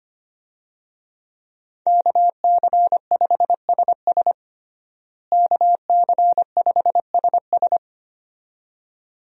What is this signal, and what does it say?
Morse code, heard as a keyed tone, about 25 wpm, 700 Hz. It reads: KC5SS KC5SS
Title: USA Call Signs - 5 Characters - Repeated 2x - 25wpm